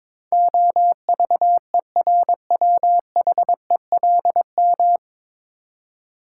Morse code, heard as a keyed tone, about 22 wpm, 700 Hz. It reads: OVERWHELM
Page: Morse Code Ninja practice 22wpm